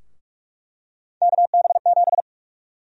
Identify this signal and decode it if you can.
Morse code, heard as a keyed tone, about 41 wpm, 700 Hz. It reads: KB7
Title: USA Call Sign Prefixes - Repeated 2x - 45wpm